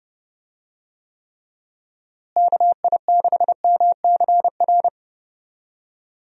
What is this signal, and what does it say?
Morse code, heard as a keyed tone, about 31 wpm, 700 Hz. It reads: KI6MCR
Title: USA Call Signs - 6 Characters - Repeated 2x - 30wpm